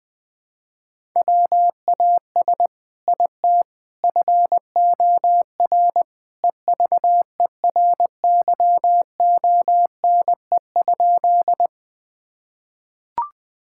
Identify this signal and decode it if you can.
Morse code, heard as a keyed tone, about 20 wpm, 700 Hz. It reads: WAS IT FOR EVERYONE?
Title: Sentences from Top 1000 Words - Review Part 5 - 20wpm